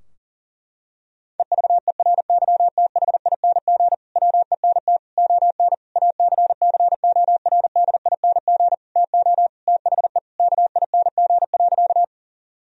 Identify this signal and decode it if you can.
Morse code, heard as a keyed tone, about 37 wpm, 700 Hz. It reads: EVERYTHING WENT ON ACCORDING TO THE KING.
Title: Sentences from Top 1000 Words - Review Part 4 - 40wpm